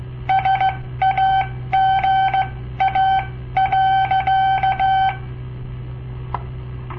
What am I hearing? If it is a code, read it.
Morse code, heard as a keyed tone, about 16 wpm, spaced as for 13 wpm, 770 Hz. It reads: SAGA.
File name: papertapes / local / Code Practice-01